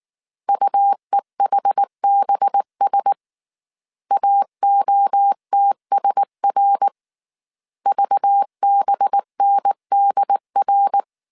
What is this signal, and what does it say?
Morse code, heard as a keyed tone, about 19 wpm, 800 Hz. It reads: UEHBS AOTSR VBNDR
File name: CW / DET / 1 6 1500C S A 18wpm 2255